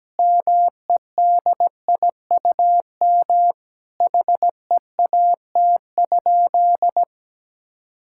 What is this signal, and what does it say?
Morse code, heard as a keyed tone, about 17 wpm, 700 Hz. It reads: MEDIUM HEAT?
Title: Sentences from Top 1000 Words - Review Part 6 - 17wpm